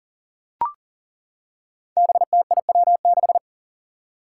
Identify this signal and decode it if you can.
Morse code, heard as a keyed tone, about 40 wpm, 700 Hz. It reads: BTIW6
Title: Five Challenging Letter-Number Combinations 40wpm